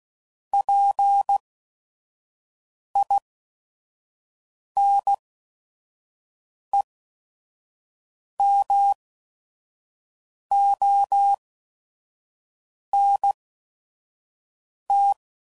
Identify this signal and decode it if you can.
Morse code, heard as a keyed tone, about 16 wpm, spaced as for 5 wpm, 800 Hz. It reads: PINEMONT
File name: CW Code Groups @5WPM / fromSM's / Cities